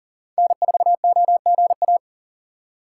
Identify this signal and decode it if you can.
Morse code, heard as a keyed tone, about 40 wpm, 700 Hz. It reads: N4OGA
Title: USA Call Signs - 5 Characters - 4x Character Spacing - 40wpm